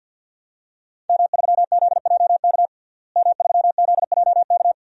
Morse code, heard as a keyed tone, about 53 wpm, 700 Hz. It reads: M3ZJK M3ZJK
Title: DXCC-100 Call Signs - 5 Characters - Repeated 2x - 50wpm